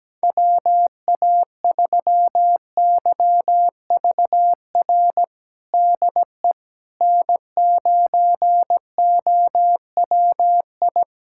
Morse code, heard as a keyed tone, about 17 wpm, 700 Hz. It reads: WA3YVR DE N9OWI
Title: USA Callsign DE Callsign 17wpm